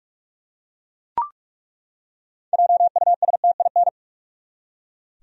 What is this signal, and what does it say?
Morse code, heard as a keyed tone, about 42 wpm, 700 Hz. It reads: JUSTIN
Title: USA Common Names - 45wpm